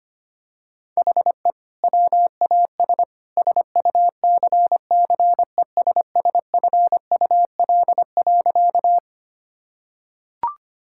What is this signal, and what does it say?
Morse code, heard as a keyed tone, about 25 wpm, 700 Hz. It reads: HE WAS SUCCESSFUL.
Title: Sentences from Top 1000 Words - Review Part 6 - 25wpm